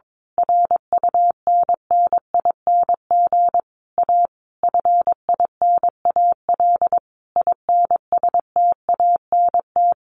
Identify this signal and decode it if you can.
Morse code, heard as a keyed tone, about 22 wpm, 700 Hz. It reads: RUNNING A FINAL INSTANT
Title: A Sound Of Thunder 22wpm